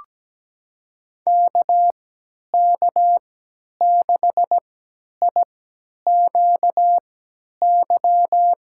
Morse code, heard as a keyed tone, about 17 wpm, spaced as for 10 wpm, 700 Hz. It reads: KK6IQY
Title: USA Call Signs - 6 Characters - 3x Character Spacing - 17wpm